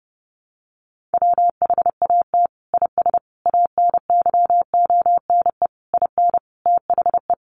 Morse code, heard as a keyed tone, about 28 wpm, 700 Hz. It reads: WHAT IS ANYONE IN THE
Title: Sentences from Top 900 Words 30wpm